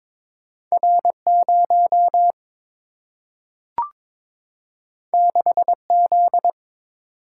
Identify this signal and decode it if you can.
Morse code, heard as a keyed tone, about 22 wpm, 700 Hz. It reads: R 0 6 Z
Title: Single-Letter Single-Number 22wpm